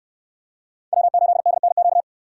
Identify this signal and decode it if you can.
Morse code, heard as a keyed tone, about 68 wpm, 700 Hz. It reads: W8DNQ